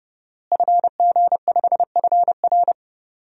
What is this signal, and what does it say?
Morse code, heard as a keyed tone, about 31 wpm, 700 Hz. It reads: FG5FR
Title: DXCC-100 Call Signs - 5 Characters - Repeated 2x - 30wpm